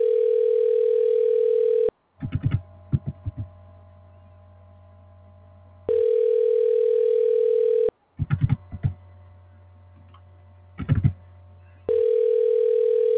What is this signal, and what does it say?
Ringback tone (North America), 440 + 480 Hz.